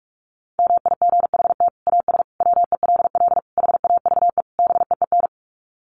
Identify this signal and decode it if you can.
Morse code, heard as a keyed tone, about 44 wpm, 700 Hz. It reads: MIGHT AS WELL HAVE BEEN
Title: Sets of 5 Words Encore2 45wpm